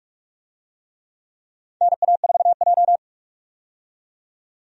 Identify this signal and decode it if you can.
Morse code, heard as a keyed tone, about 46 wpm, 700 Hz. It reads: NA4J
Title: USA Call Signs - 4 Characters - 45wpm